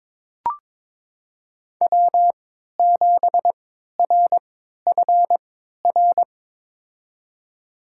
Morse code, heard as a keyed tone, about 23 wpm, spaced as for 13 wpm, 700 Hz. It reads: W7RFR
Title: USA Call Signs - 5 Characters - 3x Character Spacing - 22wpm